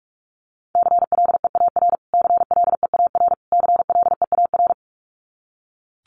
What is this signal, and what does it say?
Morse code, heard as a keyed tone, about 44 wpm, 700 Hz. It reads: CLEAR CLEAR CLEAR
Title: Top 500 Words Repeated 45wpm